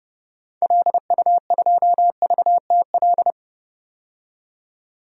Morse code, heard as a keyed tone, about 32 wpm, 700 Hz. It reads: LU2VTL
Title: DXCC-100 Call Signs - 6 Characters - 30wpm